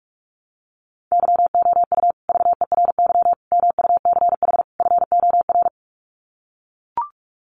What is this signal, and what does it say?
Morse code, heard as a keyed tone, about 44 wpm, 700 Hz. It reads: YOU VERY MUCH FOR
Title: Sets of 4 Words 45wpm